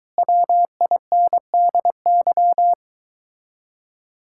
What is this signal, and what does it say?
Morse code, heard as a keyed tone, about 23 wpm, 700 Hz. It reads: WINDY